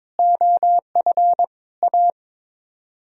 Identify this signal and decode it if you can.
Morse code, heard as a keyed tone, about 22 wpm, 700 Hz. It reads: OF A
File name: practice / 22wpm / Sets of 6 Words - 22wpm